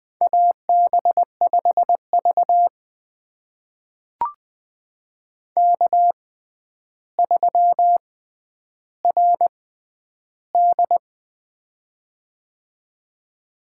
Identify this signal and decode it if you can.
Morse code, heard as a keyed tone, about 20 wpm, spaced as for 10 wpm, 700 Hz. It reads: AB5V K3RD
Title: USA Call Signs - 4 Characters - 6x Character Spacing - 20wpm